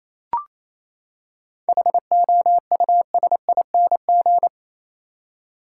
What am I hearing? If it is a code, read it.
Morse code, heard as a keyed tone, about 28 wpm, 700 Hz. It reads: HOUSING
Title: Seven Letter Words - 28wpm